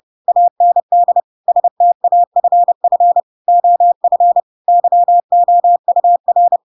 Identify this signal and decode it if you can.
Morse code, heard as a keyed tone, about 30 wpm, 700 Hz. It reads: AND STAFF OF YOUR
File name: Pinocchio / Pinocchio ch8 30 30 ik1hge